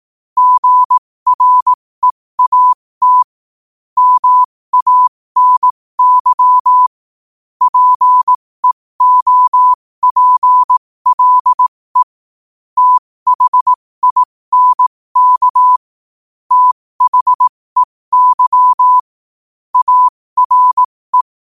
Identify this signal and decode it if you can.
Morse code, heard as a keyed tone, about 18 wpm, spaced as for 15 wpm, 1000 Hz. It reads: GREAT MANY PEOPLE THINK THEY ARE
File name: qotdfiles / 15wpm